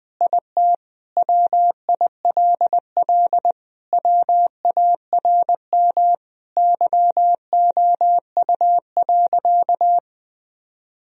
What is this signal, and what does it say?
Morse code, heard as a keyed tone, about 20 wpm, 700 Hz. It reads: IT WILL WARM YOU.
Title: Sentences from Top 1000 Words - Review Part 2 - 20wpm